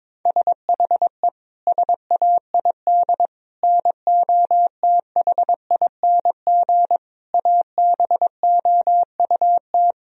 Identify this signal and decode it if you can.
Morse code, heard as a keyed tone, about 22 wpm, 700 Hz. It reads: SHE SAID NOTHING ABOUT
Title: Sentences from Top 1000 Words - Review Part 2 - 22wpm